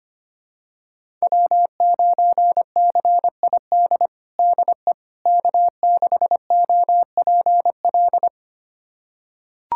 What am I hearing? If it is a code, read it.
Morse code, heard as a keyed tone, about 25 wpm, 700 Hz. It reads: W9CID DE K6OPL
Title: USA Callsign DE Callsign - 5 Characters - 25wpm